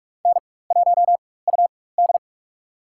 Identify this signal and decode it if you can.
Morse code, heard as a keyed tone, about 44 wpm, spaced as for 21 wpm, 700 Hz. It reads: N1UD
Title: USA Call Signs - 4 Characters - 4x Character Spacing - 45wpm